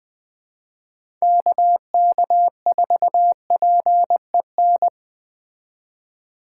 Morse code, heard as a keyed tone, about 20 wpm, 700 Hz. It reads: KK4PEN
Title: USA Call Signs - 3x Character Spacing - 20wpm